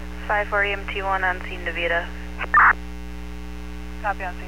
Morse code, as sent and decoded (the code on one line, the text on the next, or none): none